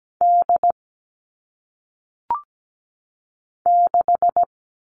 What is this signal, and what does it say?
Morse code, heard as a keyed tone, about 17 wpm, 700 Hz. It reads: D 6
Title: All Cut Numbers - 1-Digit 17wpm